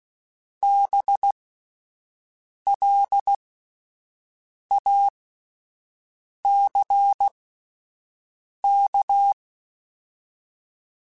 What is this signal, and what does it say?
Morse code, heard as a keyed tone, about 16 wpm, spaced as for 5 wpm, 800 Hz. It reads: BLACK